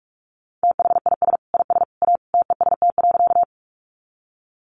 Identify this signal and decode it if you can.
Morse code, heard as a keyed tone, about 44 wpm, 700 Hz. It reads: THIS IS A TEST.
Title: Sentences from Top 500 Words 45wpm